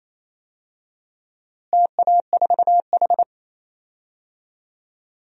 Morse code, heard as a keyed tone, about 27 wpm, 700 Hz. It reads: TA4H
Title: DXCC-100 Call Signs - 4 Characters - Repeated 3x - 28wpm